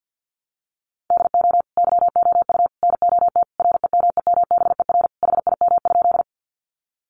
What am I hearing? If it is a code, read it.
Morse code, heard as a keyed tone, about 50 wpm, 700 Hz. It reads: DO YOU NOT REMEMBER HIM?
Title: Sentences from Top 600 Words 50wpm